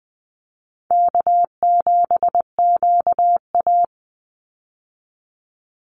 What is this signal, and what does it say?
Morse code, heard as a keyed tone, about 20 wpm, 700 Hz. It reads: K7QA
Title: Easy Call Signs - 20wpm